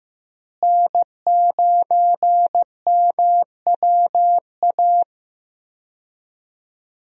Morse code, heard as a keyed tone, about 16 wpm, 700 Hz. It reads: N9MWA
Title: USA Call Signs - 5 Characters - Repeated 3x - 15wpm